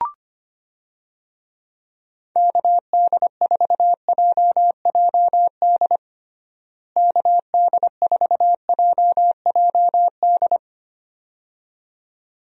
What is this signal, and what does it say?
Morse code, heard as a keyed tone, about 25 wpm, 700 Hz. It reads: KD4JJD KD4JJD
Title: USA Call Signs - 6 Characters - Repeated 2x - 25wpm